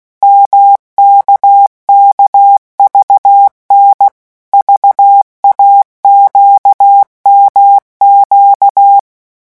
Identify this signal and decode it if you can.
Morse code, heard as a keyed tone, about 16 wpm, 800 Hz. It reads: MKKVN VAQMQ
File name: morse mp3 / revision3-vitesse 15 mots